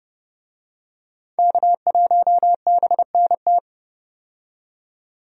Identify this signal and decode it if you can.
Morse code, heard as a keyed tone, about 30 wpm, 700 Hz. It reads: K1BNT
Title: USA Call Signs - 5 Characters - 30wpm